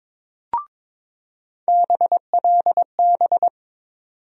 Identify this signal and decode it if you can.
Morse code, heard as a keyed tone, about 22 wpm, 700 Hz. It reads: BLB
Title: USA Call Sign Postfixes - 3 Letters 22wpm